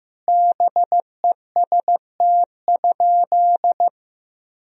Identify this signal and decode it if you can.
Morse code, heard as a keyed tone, about 15 wpm, 700 Hz. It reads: BEST?
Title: Sentences from Top 1000 Words - Review Part 2 - 15wpm